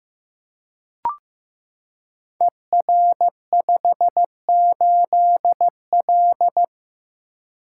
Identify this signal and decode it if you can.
Morse code, heard as a keyed tone, about 15 wpm, 700 Hz. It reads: ER58L
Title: Code Groups - 5 Letter:Numbers - 15wpm